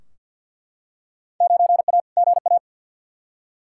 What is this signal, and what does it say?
Morse code, heard as a keyed tone, about 47 wpm, 700 Hz. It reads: 9A GA
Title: Field Day Exchanges Encore - 50wpm